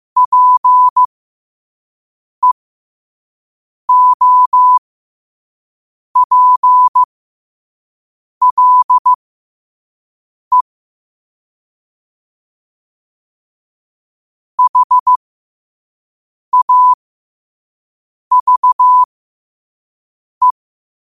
Morse code, heard as a keyed tone, about 15 wpm, spaced as for 5 wpm, 1000 Hz. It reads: PEOPLE HAVE
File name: qotdfiles / 5wpm